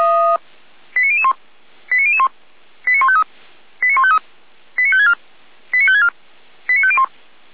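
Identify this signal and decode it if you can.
Five-tone selective call (ZVEI), heard as quick runs of five tones, tones of about 70 ms. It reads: 89001 89001 89153 89153 89564 89564 89501
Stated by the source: ZVEI five-tone selective call received by SDR.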